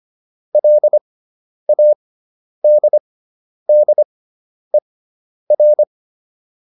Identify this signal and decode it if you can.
Morse code, heard as a keyed tone, about 25 wpm, spaced as for 10 wpm, 600 Hz. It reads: LADDER